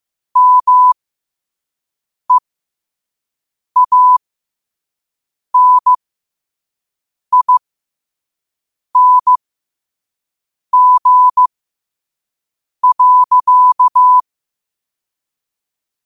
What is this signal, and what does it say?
Morse code, heard as a keyed tone, about 15 wpm, spaced as for 5 wpm, 1000 Hz. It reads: MEANING.